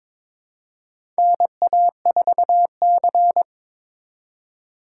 Morse code, heard as a keyed tone, about 22 wpm, 700 Hz. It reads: NA4C
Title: K1USN SST QSO Practice - 22wpm